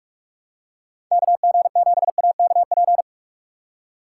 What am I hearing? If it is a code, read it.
Morse code, heard as a keyed tone, about 46 wpm, 700 Hz. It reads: KG7AKP
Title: USA Call Signs - 6 Characters - 3x Character Spacing - 45wpm